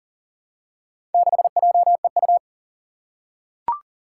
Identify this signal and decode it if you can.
Morse code, heard as a keyed tone, about 38 wpm, 700 Hz. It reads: 6JEU
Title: Four Challenging Letter-Number Combinations 40wpm